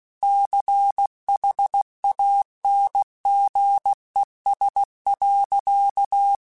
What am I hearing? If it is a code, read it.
Morse code, heard as a keyed tone, about 16 wpm, 800 Hz. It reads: CHANGES.
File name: Technical Subjects / Technical Subjects-mp3s / XL @15